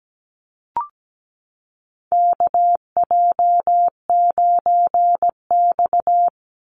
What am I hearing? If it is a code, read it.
Morse code, heard as a keyed tone, about 17 wpm, 700 Hz. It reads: KJ9X
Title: NA SOTA Chaser Callsigns 17wpm